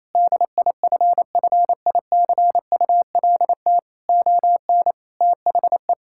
Morse code, heard as a keyed tone, about 28 wpm, 700 Hz. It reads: DIFFICULT ON THE
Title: Sentences from Top 1000 Words - Review Part 5 - 28wpm